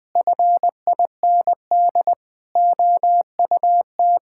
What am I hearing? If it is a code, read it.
Morse code, heard as a keyed tone, about 20 wpm, 700 Hz. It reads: FIND OUT
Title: Sentences from Top 1000 Words - Review Part 5 - 20wpm